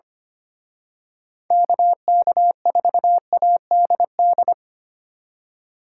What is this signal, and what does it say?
Morse code, heard as a keyed tone, about 31 wpm, 700 Hz. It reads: KK4ADD
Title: Speed Racing - USA 2x3 Call Signs 25wpm